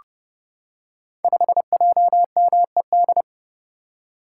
Morse code, heard as a keyed tone, about 31 wpm, 700 Hz. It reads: 5JMED
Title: Code Groups - 5 Letter:Numbers - 30wpm